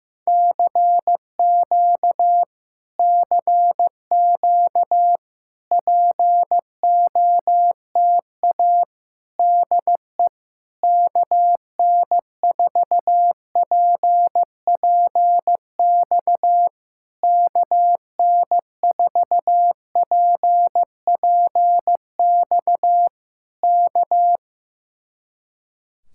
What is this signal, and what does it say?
Morse code, heard as a keyed tone, about 15 wpm, 700 Hz. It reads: CQ CQ POTA DE KN4PPX KN4PPX K
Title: Speed-Racing Common QSO Phrases Encore - 15wpm